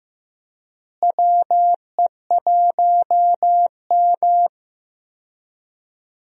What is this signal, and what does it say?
Morse code, heard as a keyed tone, about 15 wpm, 700 Hz. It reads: WE1M
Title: USA Call Signs - 4 Characters - Repeated 3x - 15wpm